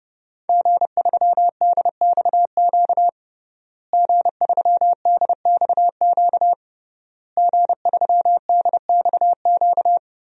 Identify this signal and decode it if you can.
Morse code, heard as a keyed tone, about 30 wpm, 700 Hz. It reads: G3DXQ G3DXQ G3DXQ
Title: DXCC-100 Call Signs - 5 Characters - Repeated 3x - 30wpm